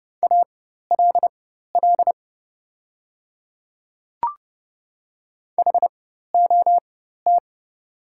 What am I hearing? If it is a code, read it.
Morse code, heard as a keyed tone, about 30 wpm, spaced as for 14 wpm, 700 Hz. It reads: ALL HOT
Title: 4x Character Spacing - Top 200 Words - 30wpm